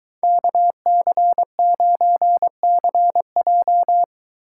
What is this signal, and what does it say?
Morse code, heard as a keyed tone, about 23 wpm, 700 Hz. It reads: KC9CJ